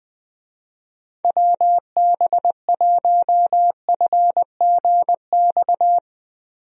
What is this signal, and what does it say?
Morse code, heard as a keyed tone, about 20 wpm, 700 Hz. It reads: WB1FGX
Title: USA Call Signs - 6 Characters - Repeated 3x - 20wpm